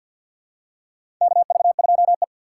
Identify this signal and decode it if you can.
Morse code, heard as a keyed tone, about 50 wpm, 700 Hz. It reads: KV2E
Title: USA Call Signs - 4 Characters - 3x Character Spacing - 50wpm